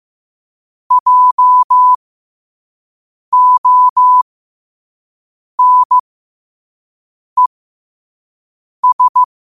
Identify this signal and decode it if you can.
Morse code, heard as a keyed tone, about 15 wpm, spaced as for 5 wpm, 1000 Hz. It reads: JONES
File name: qotdfiles / 5wpm